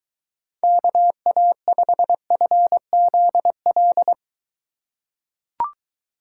Morse code, heard as a keyed tone, about 23 wpm, 700 Hz. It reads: KA5FZL